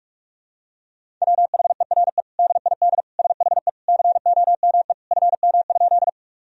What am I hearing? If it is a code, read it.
Morse code, heard as a keyed tone, about 47 wpm, 700 Hz. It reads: WHERE DID SHE COME FM?
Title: Sentences from Top 200 Words and Abbreviations 45wpm